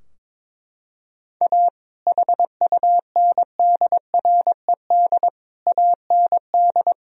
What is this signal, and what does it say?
Morse code, heard as a keyed tone, about 21 wpm, 700 Hz. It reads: A HUNDRED AND
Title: Sentences from Top 1000 Words - Review Part 2 - 22wpm